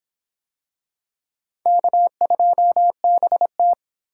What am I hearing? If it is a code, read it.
Morse code, heard as a keyed tone, about 26 wpm, 700 Hz. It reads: K2BT